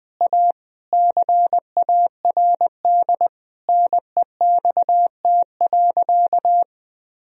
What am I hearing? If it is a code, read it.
Morse code, heard as a keyed tone, about 20 wpm, 700 Hz. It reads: A CARD NEXT.
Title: Sentences from Top 1000 Words - Review Part 4 - 20wpm